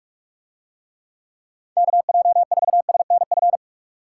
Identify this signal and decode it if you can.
Morse code, heard as a keyed tone, about 45 wpm, 700 Hz. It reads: KJ4SNF